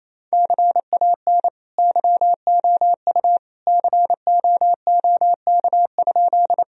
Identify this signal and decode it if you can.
Morse code, heard as a keyed tone, about 28 wpm, 700 Hz. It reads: CAN YOU COOK?
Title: Sentences from Top 1000 Words - Review Part 4 - 28wpm